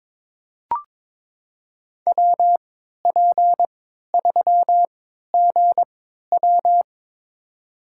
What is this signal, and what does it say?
Morse code, heard as a keyed tone, about 22 wpm, spaced as for 13 wpm, 700 Hz. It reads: WP3GW
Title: USA Call Signs - 5 Characters - 3x Character Spacing - 22wpm